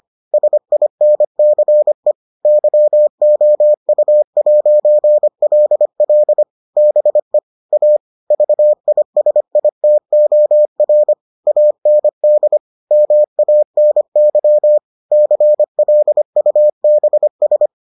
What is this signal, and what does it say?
Morse code, heard as a keyed tone, about 25 wpm, 600 Hz. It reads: SINCE YOU'LL BE A VISITOR AND MANY CLUBS